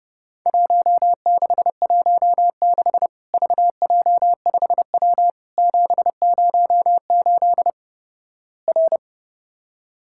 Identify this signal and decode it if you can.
Morse code, heard as a keyed tone, about 32 wpm, 700 Hz. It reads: 1616 VJ5W 708 R
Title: Worked All Europe DX Contest - QTC Exchange - 30wpm